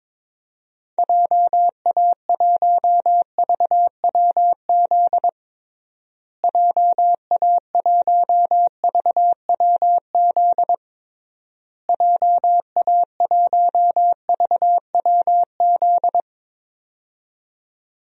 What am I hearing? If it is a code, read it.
Morse code, heard as a keyed tone, about 22 wpm, 700 Hz. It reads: JA1VWZ JA1VWZ JA1VWZ
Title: DXCC-100 Call Signs - 6 Characters - Repeated 3x - 22wpm